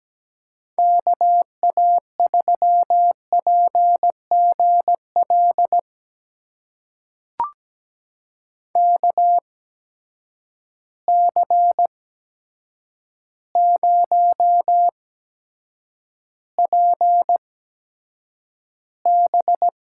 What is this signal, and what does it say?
Morse code, heard as a keyed tone, about 17 wpm, spaced as for 8 wpm, 700 Hz. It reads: KA3PGL KC0PB
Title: USA Call Signs - 6 Characters - 8x Character Spacing - 17wpm